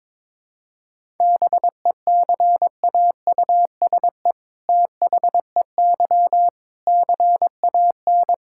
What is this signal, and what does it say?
Morse code, heard as a keyed tone, about 22 wpm, 700 Hz. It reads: BECAUSE THEY CAN
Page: morse-code-ninja-practice-22wpm